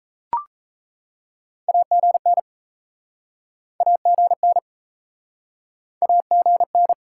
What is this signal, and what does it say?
Morse code, heard as a keyed tone, about 39 wpm, 700 Hz. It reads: AGN AGN AGN